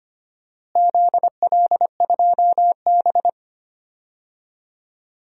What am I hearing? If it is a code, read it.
Morse code, heard as a keyed tone, about 25 wpm, 700 Hz. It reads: ZL2B
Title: DXCC-100 Call Signs - 4 Characters - Repeated 2x - 25wpm